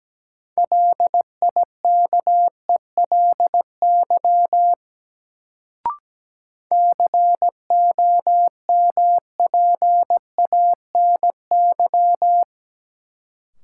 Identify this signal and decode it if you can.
Morse code, heard as a keyed tone, about 17 wpm, 700 Hz. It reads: LIKELY COMPANY
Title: Sentences from Top 600 Words - New Word Drill - 17wpm